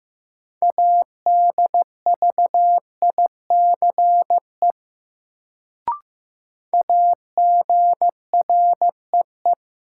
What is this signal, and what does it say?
Morse code, heard as a keyed tone, about 15 wpm, 700 Hz. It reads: ADVICE AGREE
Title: Sentences from Top 1100 Words - New Word Drill Alt - 15wpm